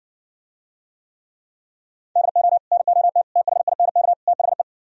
Morse code, heard as a keyed tone, about 70 wpm, 700 Hz. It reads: DO NOT THINK THE